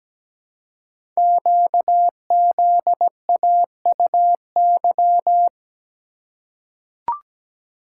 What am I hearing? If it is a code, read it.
Morse code, heard as a keyed tone, about 17 wpm, 700 Hz. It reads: QZAUY
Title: Code Groups - 5 Letters - 17wpm